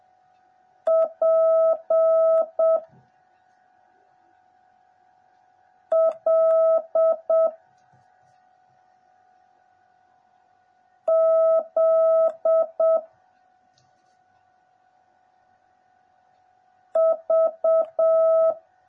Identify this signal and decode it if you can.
Morse code, heard as a keyed tone, about 7 wpm, 650 Hz. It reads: P L Z V
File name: jh8cbh / 2021cw / cw09a